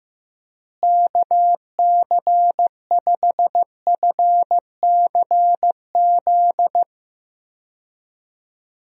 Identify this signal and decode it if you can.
Morse code, heard as a keyed tone, about 16 wpm, 700 Hz. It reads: KC5FCZ
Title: USA Call Signs - 6 Characters - Repeated 2x - 15wpm